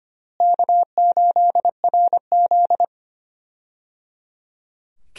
Morse code, heard as a keyed tone, about 25 wpm, 700 Hz. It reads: K8RZ
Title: USA Call Signs - 4 Characters - Repeated 3x - 25wpm